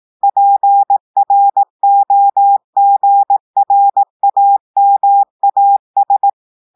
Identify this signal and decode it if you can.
Morse code, heard as a keyed tone, about 18 wpm, 800 Hz. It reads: PROGRAMAS
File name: audios / portugais / portugais texte en clair 2 vitesse 18 mots minutes